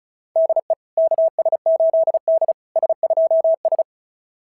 Morse code, heard as a keyed tone, about 36 wpm, 660 Hz. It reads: DE KS8D S2S
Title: SOTA QSO Practice 35wpm